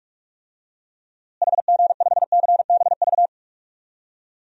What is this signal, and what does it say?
Morse code, heard as a keyed tone, about 41 wpm, 700 Hz. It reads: HG5CBV
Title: DXCC-100 Call Signs - 6 Characters - Repeated 2x - 45wpm